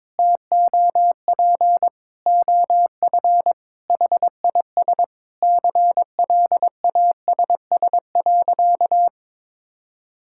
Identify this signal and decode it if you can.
Morse code, heard as a keyed tone, about 21 wpm, 700 Hz. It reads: TOP OF HIS CLASS.
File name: practice / 22wpm / Sentences with 4-Word Phrases - 22wpm